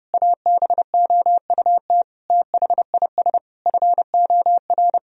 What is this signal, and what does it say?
Morse code, heard as a keyed tone, about 30 wpm, 700 Hz. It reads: ABOUT THIS FOR
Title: Sentences from Top 1000 Words - Review Part 1 - 30wpm